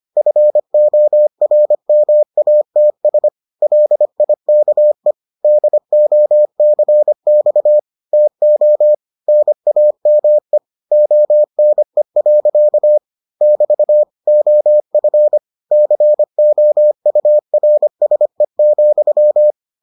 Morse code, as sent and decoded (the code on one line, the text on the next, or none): ..-. --- .-. -- .- - ... / .-.. .. -.- . / -.. --- -.-. -..- / - --- / -. .- -- . / --- -. .-.-.- / -...- / --- ..-. / -.-. --- ..- .-. ... --..--
FORMATS LIKE DOCX TO NAME ONE. <BT> OF COURSE,